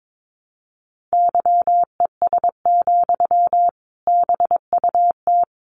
Morse code, heard as a keyed tone, about 22 wpm, 700 Hz. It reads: YES, BUT
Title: Sentences from Top 500 Words 22wpm